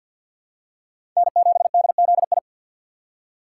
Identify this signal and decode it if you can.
Morse code, heard as a keyed tone, about 49 wpm, 700 Hz. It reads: N7DZI
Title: USA Call Signs - 2x Character Spacing - 50wpm